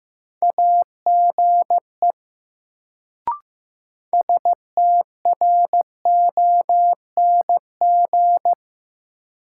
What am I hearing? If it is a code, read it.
Morse code, heard as a keyed tone, about 15 wpm, 700 Hz. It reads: AGE STRONG
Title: Sentences from Top 500 Words - New Word Drill - 15wpm